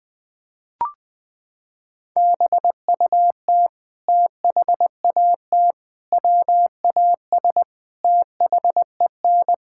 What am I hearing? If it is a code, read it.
Morse code, heard as a keyed tone, about 20 wpm, 700 Hz. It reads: BUT THAT WAS THEN
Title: Sets of 8 Words - 20wpm